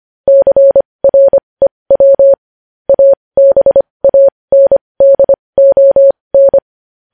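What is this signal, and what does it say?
Morse code, heard as a keyed tone, about 25 wpm, spaced as for 20 wpm, 550 Hz. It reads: CREW ABANDON
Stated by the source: Morse news headlines sent at 20.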